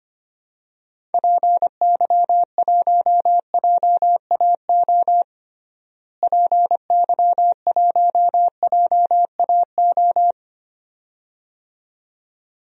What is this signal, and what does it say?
Morse code, heard as a keyed tone, about 25 wpm, 700 Hz. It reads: PY1JAO PY1JAO
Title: DXCC-100 Call Signs - 6 Characters - Repeated 2x - 25wpm